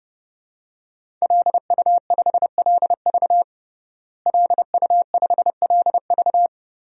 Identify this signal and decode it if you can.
Morse code, heard as a keyed tone, about 32 wpm, 700 Hz. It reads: LU5LV LU5LV
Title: DXCC-100 Call Signs - 5 Characters - Repeated 3x - 30wpm